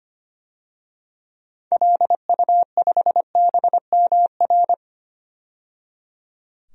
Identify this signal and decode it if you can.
Morse code, heard as a keyed tone, about 25 wpm, 700 Hz. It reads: LU5BMR
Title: DXCC-100 Call Signs - 6 Characters - 25wpm